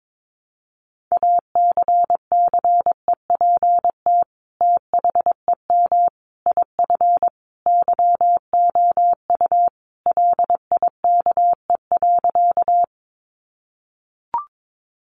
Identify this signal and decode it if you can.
Morse code, heard as a keyed tone, about 22 wpm, 700 Hz. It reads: ACCEPT THEM IF YOU LIKE.
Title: Sentences from Top 1300 Words 22wpm